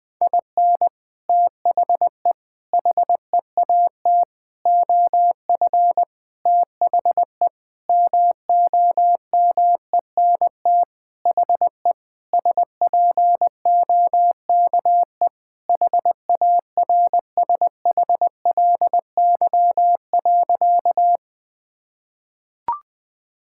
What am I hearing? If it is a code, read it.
Morse code, heard as a keyed tone, about 20 wpm, 700 Hz. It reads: IN THE HEAT OF THE MOMENT HE SPOKE HARSHLY.
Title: Sentences with 6-Word Phrases - 20wpm